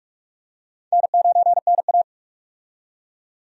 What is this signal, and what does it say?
Morse code, heard as a keyed tone, about 46 wpm, 700 Hz. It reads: N9NA
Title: USA Call Signs - 4 Characters - 45wpm